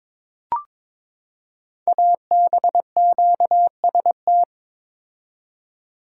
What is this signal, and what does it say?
Morse code, heard as a keyed tone, about 22 wpm, 700 Hz. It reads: ABQST
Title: Code Groups - 5 Letters - 22wpm